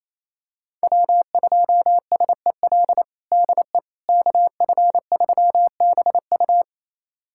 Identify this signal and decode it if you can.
Morse code, heard as a keyed tone, about 28 wpm, 700 Hz. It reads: W2SEL DE KF3BU
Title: USA Callsign DE Callsign - 5 Characters - 28wpm